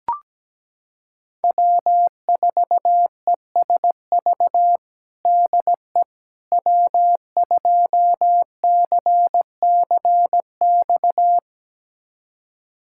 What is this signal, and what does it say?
Morse code, heard as a keyed tone, about 17 wpm, 700 Hz. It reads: W4ESV DE W2CCX
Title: USA Callsign DE Callsign - 5 Characters - 17wpm